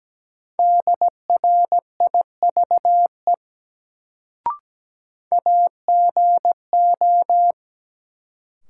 Morse code, heard as a keyed tone, about 17 wpm, 700 Hz. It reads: DRIVE AGO
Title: Sentences from Top 500 Words - New Word Drill - 17wpm